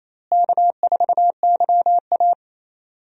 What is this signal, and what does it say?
Morse code, heard as a keyed tone, about 28 wpm, 700 Hz. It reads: K4YA
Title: USA Call Signs - 4 Characters - Repeated 2x - 28wpm